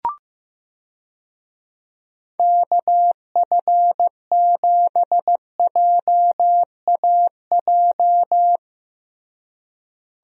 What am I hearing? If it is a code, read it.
Morse code, heard as a keyed tone, about 15 wpm, 700 Hz. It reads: KF7JAJ
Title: USA Call Signs - 6 Characters - Repeated 2x - 15wpm